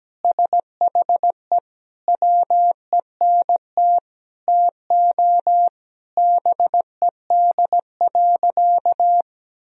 Morse code, heard as a keyed tone, about 17 wpm, 700 Hz. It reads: SHE WENT TO BED.
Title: Sentences from Top 1000 Words - Review Part 3 - 17wpm